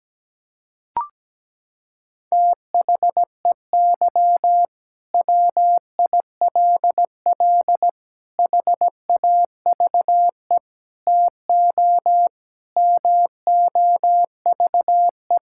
Morse code, heard as a keyed tone, about 17 wpm, 700 Hz. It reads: THEY WILL HAVE TO MOVE
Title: Sentences from Top 1000 Words - Review Part 6 - 17wpm